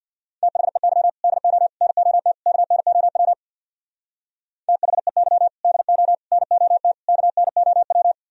Morse code, heard as a keyed tone, about 64 wpm, 700 Hz. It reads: THEY DO NOT KNOW THEY DO NOT KNOW